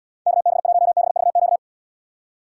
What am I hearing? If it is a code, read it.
Morse code, heard as a keyed tone, about 78 wpm, 700 Hz. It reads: KB1BVQ